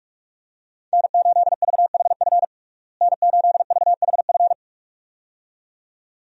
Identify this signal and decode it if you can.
Morse code, heard as a keyed tone, about 46 wpm, 700 Hz. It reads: N8VHF N8VHF
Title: USA Call Signs - 5 Characters - Repeated 2x - 45wpm